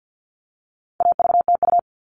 Morse code, heard as a keyed tone, about 50 wpm, 700 Hz. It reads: AVTU
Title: All Cut Numbers - 4-Digit 50wpm